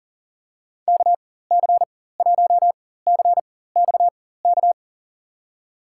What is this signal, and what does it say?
Morse code, heard as a keyed tone, about 38 wpm, spaced as for 18 wpm, 700 Hz. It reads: KC1CXK
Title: USA Call Signs - 6 Characters - 4x Character Spacing - 40wpm